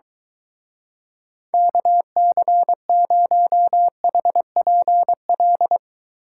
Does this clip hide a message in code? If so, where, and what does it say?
Yes: Morse code from 1.5 to 5.8 s, heard as a keyed tone, about 23 wpm, 700 Hz. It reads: KC0HPL